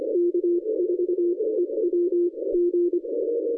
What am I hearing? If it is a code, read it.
Morse code, heard as a keyed tone, about 24 wpm, spaced as for 20 wpm, 340 Hz, with radio noise. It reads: K4EWG
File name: 2015 cq ww cw / 2015 cqwww k5tr - 0-36